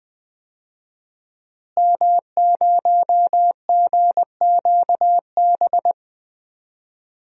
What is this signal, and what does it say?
Morse code, heard as a keyed tone, about 21 wpm, 700 Hz. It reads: M0GQB
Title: DXCC-100 Call Signs - 5 Characters - Repeated 3x - 20wpm